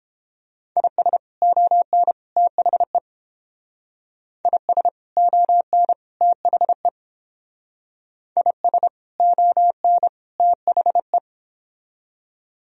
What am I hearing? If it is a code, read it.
Morse code, heard as a keyed tone, about 30 wpm, 700 Hz. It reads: IS ON THE IS ON THE IS ON THE